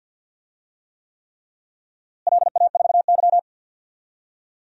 Morse code, heard as a keyed tone, about 48 wpm, 700 Hz. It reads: LA4Y
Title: DXCC-100 Call Signs - 4 Characters - Repeated 3x - 50wpm